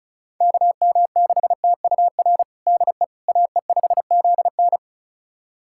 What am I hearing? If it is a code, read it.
Morse code, heard as a keyed tone, about 34 wpm, 700 Hz. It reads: KM6TUR DE AE5ZN
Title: USA Callsign DE Callsign 35wpm